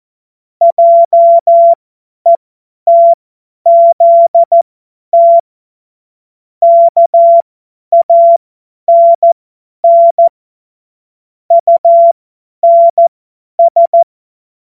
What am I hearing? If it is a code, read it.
Morse code, heard as a keyed tone, about 14 wpm, spaced as for 10 wpm, 680 Hz. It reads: JETZT KANN UNS
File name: Morsekurs / Witze / Witz 20. Juni 2023 50 F70